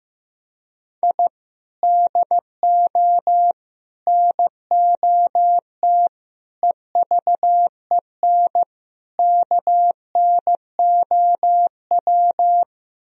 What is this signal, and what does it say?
Morse code, heard as a keyed tone, about 15 wpm, 700 Hz. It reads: I DO NOT EVEN KNOW
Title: Speed-Racing - Sets of 5 Words 15wpm